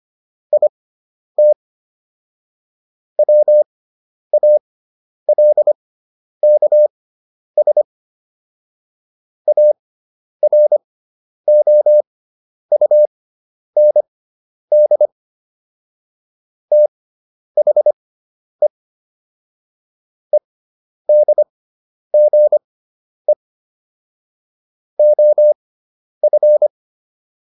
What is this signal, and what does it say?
Morse code, heard as a keyed tone, about 25 wpm, spaced as for 10 wpm, 600 Hz. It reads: IT WALKS AROUND THE EDGE OF